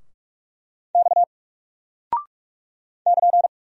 Morse code, heard as a keyed tone, about 44 wpm, 700 Hz. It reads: X <KN>
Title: Single Character (Letter:Number:Punctuation:Prosign) Alt 45wpm